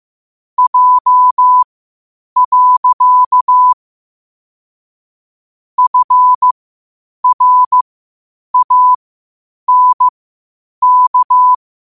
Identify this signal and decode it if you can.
Morse code, heard as a keyed tone, about 15 wpm, spaced as for 8 wpm, 1000 Hz. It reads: J. FRANK
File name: qotdfiles / 8wpm